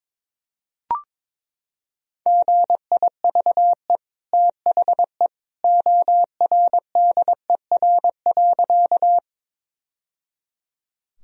Morse code, heard as a keyed tone, about 22 wpm, 700 Hz. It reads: GIVE THE ORDER.
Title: Sentences from Top 1000 Words - Review Part 2 - 22wpm